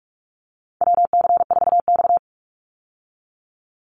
Morse code, heard as a keyed tone, about 44 wpm, 700 Hz. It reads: WC4X
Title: Call Signs Repeated 3x - 45wpm